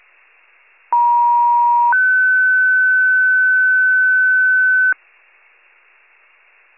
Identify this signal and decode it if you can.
Two-tone page, heard as a one-second tone, then a three-second tone, about 950 then 1500 Hz. It is heard from 0.9 to 4.9 s.